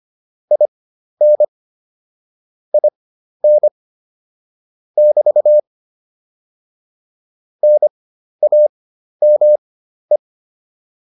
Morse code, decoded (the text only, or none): IN IN <BT> NAME